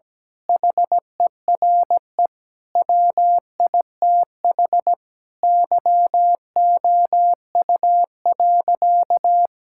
Morse code, heard as a keyed tone, about 17 wpm, 700 Hz. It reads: HERE WITH YOU.
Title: Sentences from Top 1600 Words 17wpm